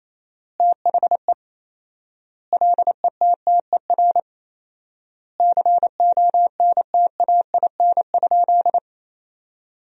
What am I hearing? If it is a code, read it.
Morse code, heard as a keyed tone, about 28 wpm, 700 Hz. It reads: THE LETTER CONTAIN?
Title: 4x Word-Spacing - Sentences from Top 500 Words - 28wpm